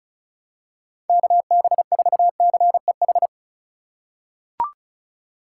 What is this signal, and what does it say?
Morse code, heard as a keyed tone, about 36 wpm, 700 Hz. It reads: KB4CEH
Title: USA Call Signs - 6 Characters - 2x Character Spacing - 35wpm